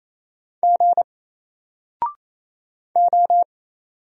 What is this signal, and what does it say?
Morse code, heard as a keyed tone, about 28 wpm, 700 Hz. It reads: G O
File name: Single-Letter-Number-28wpm